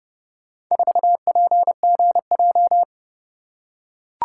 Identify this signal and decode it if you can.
Morse code, heard as a keyed tone, about 30 wpm, 700 Hz. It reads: 4PGJ